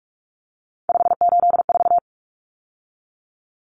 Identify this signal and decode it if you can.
Morse code, heard as a keyed tone, about 44 wpm, 700 Hz. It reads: 584
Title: Numbers - 3-Digit 45wpm